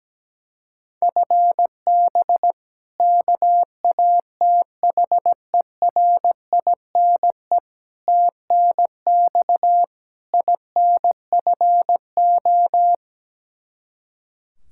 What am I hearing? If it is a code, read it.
Morse code, heard as a keyed tone, about 17 wpm, 700 Hz. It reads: FB KATHERINE TNX INFO
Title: Practice QSO - Exchange 3 - More Info - 17wpm